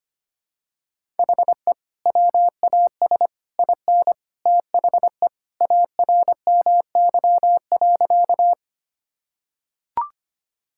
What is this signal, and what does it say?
Morse code, heard as a keyed tone, about 25 wpm, 700 Hz. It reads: HE WAS IN THE ARMY.
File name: practice / 25wpm / Sentences with 3-Word Phrases - 25wpm